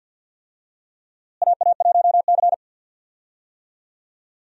Morse code, heard as a keyed tone, about 51 wpm, 700 Hz. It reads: AA1C
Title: USA Call Signs - 4 Characters - Repeated 2x - 50wpm